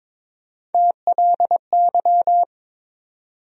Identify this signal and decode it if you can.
Morse code, heard as a keyed tone, about 22 wpm, 700 Hz. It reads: TLY
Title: USA Call Sign Postfixes - 3 Letters 22wpm